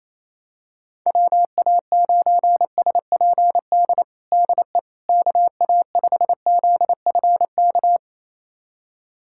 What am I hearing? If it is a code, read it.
Morse code, heard as a keyed tone, about 28 wpm, 700 Hz. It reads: WA9SPD DE KA5ZFK
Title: USA Callsign DE Callsign - 6 Characters - 28wpm